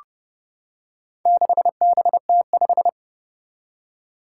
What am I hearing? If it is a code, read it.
Morse code, heard as a keyed tone, about 29 wpm, 700 Hz. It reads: 6BT5
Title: Four Challenging Letter-Number Combinations 30wpm